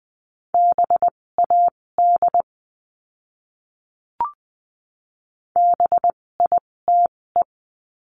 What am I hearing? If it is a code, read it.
Morse code, heard as a keyed tone, about 20 wpm, spaced as for 16 wpm, 700 Hz. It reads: BAD BITE
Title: 256 Lesson 42a - Words 12-20wpm